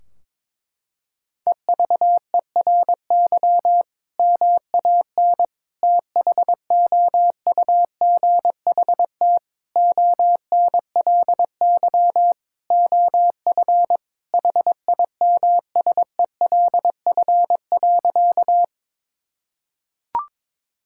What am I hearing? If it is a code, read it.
Morse code, heard as a keyed tone, about 22 wpm, 700 Hz. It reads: EVERY MAN THOUGHT ONLY OF HIMSELF.
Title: Sentences from Top 1600 Words 22wpm